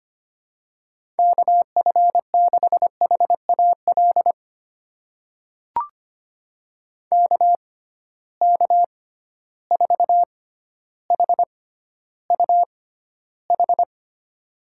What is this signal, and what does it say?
Morse code, heard as a keyed tone, about 25 wpm, spaced as for 13 wpm, 700 Hz. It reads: KF6HAL KK4HUH
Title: USA Call Signs - 6x Character Spacing - 25wpm